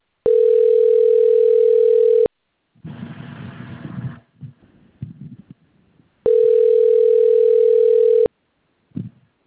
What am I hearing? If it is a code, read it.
Ringback tone (North America), 440 + 480 Hz.